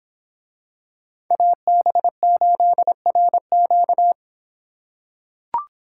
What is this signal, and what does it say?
Morse code, heard as a keyed tone, about 26 wpm, 700 Hz. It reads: AB8RQ